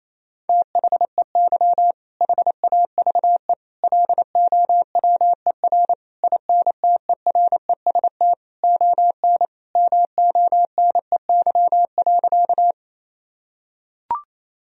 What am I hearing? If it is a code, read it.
Morse code, heard as a keyed tone, about 28 wpm, 700 Hz. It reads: THEY HAVE LOWER INTEREST ON MONEY.
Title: Sentences from Top 1000 Words - Review Part 4 - 28wpm